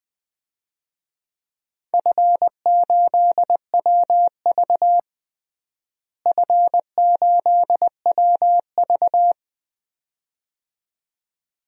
Morse code, heard as a keyed tone, about 20 wpm, 700 Hz. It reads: F8WV F8WV
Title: DXCC-100 Call Signs - 4 Characters - Repeated 2x - 20wpm